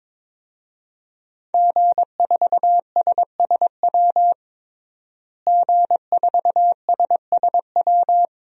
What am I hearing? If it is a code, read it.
Morse code, heard as a keyed tone, about 23 wpm, 700 Hz. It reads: G4SSW G4SSW
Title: DXCC-100 Call Signs - 5 Characters - Repeated 3x - 22wpm